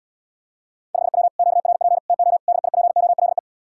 Morse code, heard as a keyed tone, about 80 wpm, 700 Hz. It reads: 5W QRP TO DIPOLE